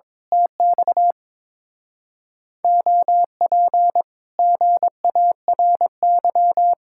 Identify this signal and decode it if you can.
Morse code, heard as a keyed tone, about 26 wpm, 700 Hz. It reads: TX OP GARY
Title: Speed-Racing Common QSO Phrases - 22wpm